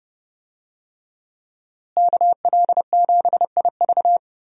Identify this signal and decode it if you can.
Morse code, heard as a keyed tone, about 31 wpm, 700 Hz. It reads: KL7IV
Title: USA Call Signs - 5 Characters - Repeated 3x - 30wpm